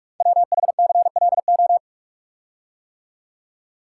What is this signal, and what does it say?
Morse code, heard as a keyed tone, about 44 wpm, 700 Hz. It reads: WHCLO
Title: Code Groups - 5 Letter:Numbers - 45wpm